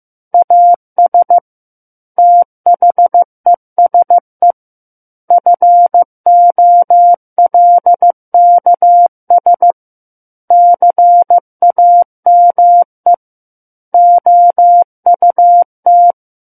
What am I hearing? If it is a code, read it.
Morse code, heard as a keyed tone, about 15 wpm, 700 Hz. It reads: AS THESE FOLKS CAME OUT
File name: Book One / 06-Chapter 6 - 15 WPM